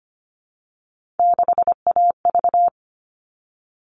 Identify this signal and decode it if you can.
Morse code, heard as a keyed tone, about 25 wpm, 700 Hz. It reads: 6AV
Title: All Cut Numbers - 3-Digit 25wpm